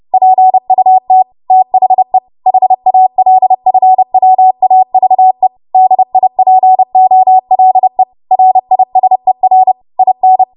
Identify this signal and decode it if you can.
Morse code, heard as a keyed tone, about 30 wpm, 750 Hz. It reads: PUT THE HALFWAVE DIPOLE RISER IN
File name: Morse / 30WPM